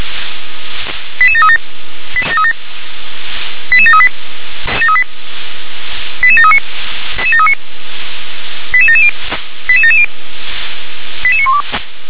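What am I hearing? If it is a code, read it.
Five-tone selective call (ZVEI), heard as quick runs of five tones, tones of about 70 ms. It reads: 88737 88738 88739 88770 88770 88912
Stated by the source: ZVEI five-tone selective call received on an SDR.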